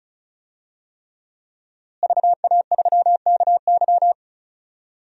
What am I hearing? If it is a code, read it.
Morse code, heard as a keyed tone, about 34 wpm, 700 Hz. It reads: VA3KY
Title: DXCC-100 Call Signs - 5 Characters - Repeated 3x - 35wpm